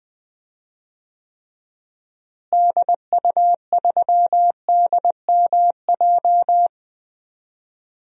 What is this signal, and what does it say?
Morse code, heard as a keyed tone, about 20 wpm, 700 Hz. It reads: DU3DMJ
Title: DXCC-100 Call Signs - 6 Characters - Repeated 2x - 20wpm